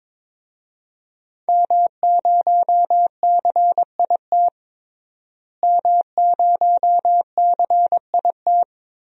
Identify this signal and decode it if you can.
Morse code, heard as a keyed tone, about 21 wpm, 700 Hz. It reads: M0CIT M0CIT
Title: DXCC-100 Call Signs - 5 Characters - Repeated 3x - 22wpm